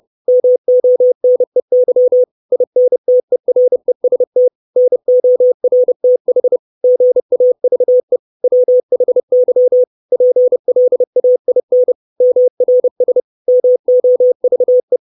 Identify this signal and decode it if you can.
Morse code, heard as a keyed tone, about 30 wpm, 500 Hz. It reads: MONEY INTEREST NORTH GAVE WHY PLAIN MRS MOVE
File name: 632 Words 30WPM